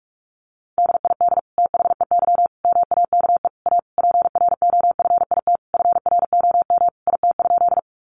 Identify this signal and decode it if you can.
Morse code, heard as a keyed tone, about 44 wpm, 700 Hz. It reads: DID THEY MAKE A PROFIT FROM IT?
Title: Sentences from Top 800 Words 45wpm